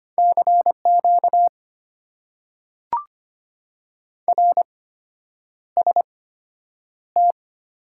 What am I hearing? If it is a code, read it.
Morse code, heard as a keyed tone, about 25 wpm, spaced as for 9 wpm, 700 Hz. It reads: CQ RST
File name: Top 90 QSO Elements - 8x Character Spacing - 25wpm